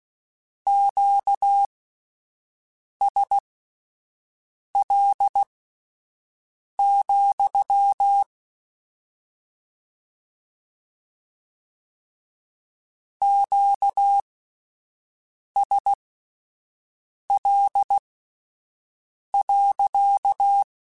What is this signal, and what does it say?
Morse code, heard as a keyed tone, about 16 wpm, spaced as for 5 wpm, 800 Hz. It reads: QSL, QSL.
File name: Q Calls / Q Calls-mp3s / Q Calls sppc @10